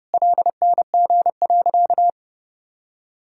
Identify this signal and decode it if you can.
Morse code, heard as a keyed tone, about 30 wpm, 700 Hz. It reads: LNG.
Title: Sentences from Top 500 Words and Abbreviations 30wpm